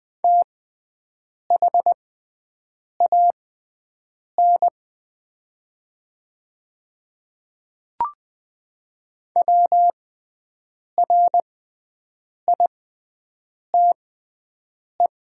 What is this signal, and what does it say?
Morse code, heard as a keyed tone, about 20 wpm, spaced as for 7 wpm, 700 Hz. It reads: THAN WRITE
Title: 6x Character Spacing - Top 100 Words - 20wpm